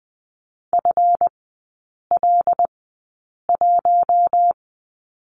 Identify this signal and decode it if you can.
Morse code, heard as a keyed tone, about 20 wpm, spaced as for 8 wpm, 700 Hz. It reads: FL1
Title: 152 Lesson 25e - Letter p - Three Characters 20wpm